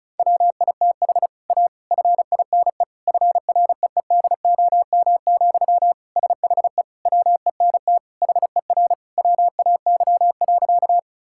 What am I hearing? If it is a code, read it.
Morse code, heard as a keyed tone, about 36 wpm, 700 Hz. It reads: WITH A FINE FREEDOM, SHE WENT HER WAY.
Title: Sentences from Top 1000 Words - Review Part 5 - 35wpm